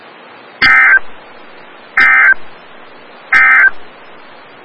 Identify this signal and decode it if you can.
Emergency Alert System (SAME) tones, heard as harsh buzzing bursts. It is heard from 0.5 to 3.8 s.
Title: Des Moines, IA NOAA Weather Radio - WXL57